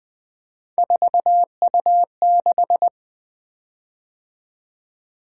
Morse code, heard as a keyed tone, about 20 wpm, 700 Hz. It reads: 4U6
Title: Three Letter-Number Combinations - 20wpm